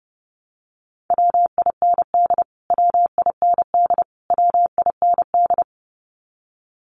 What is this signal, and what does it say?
Morse code, heard as a keyed tone, about 30 wpm, 700 Hz. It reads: WIND WIND WIND
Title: Top 500 Words Repeated 30wpm